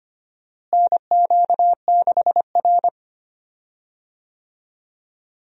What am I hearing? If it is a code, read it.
Morse code, heard as a keyed tone, about 25 wpm, 700 Hz. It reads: NQ6R
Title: USA Call Signs - 4 Characters - Repeated 2x - 25wpm